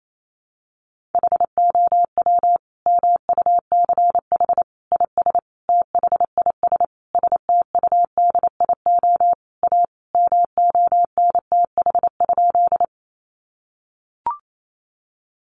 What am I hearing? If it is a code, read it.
Morse code, heard as a keyed tone, about 28 wpm, 700 Hz. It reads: HOW MUCH IS THIS STUDIO A MONTH?
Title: Sentences from Top 1200 Words 28wpm